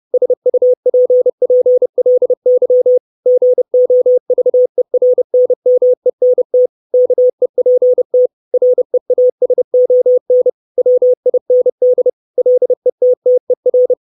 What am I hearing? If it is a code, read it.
Morse code, heard as a keyed tone, about 30 wpm, 500 Hz. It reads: SUPPLY GOVERNMENT KEPT REASON WIND LETTER
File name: 632 Words 30WPM